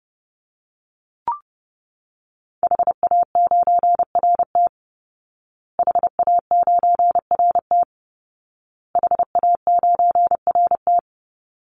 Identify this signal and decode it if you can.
Morse code, heard as a keyed tone, about 28 wpm, 700 Hz. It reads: HA9RT HA9RT HA9RT
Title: Call Signs Repeated 2x - 30wpm